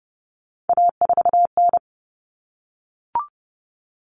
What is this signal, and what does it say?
Morse code, heard as a keyed tone, about 30 wpm, 700 Hz. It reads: A4N
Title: Common Cut Numbers - 3-Digit 30wpm